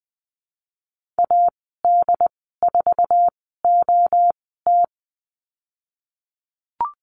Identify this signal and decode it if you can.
Morse code, heard as a keyed tone, about 20 wpm, spaced as for 15 wpm, 700 Hz. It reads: AD4OT
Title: 122 Lesson 20g - Number 3 - Call Signs 20wpm